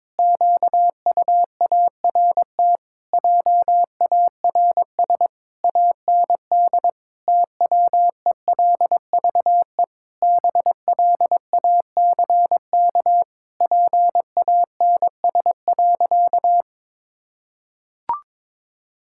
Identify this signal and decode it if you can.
Morse code, heard as a keyed tone, about 22 wpm, 700 Hz. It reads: QUART JARS AND TWELVE BLACK PANS.
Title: Pangram Challenge v2 - 22wpm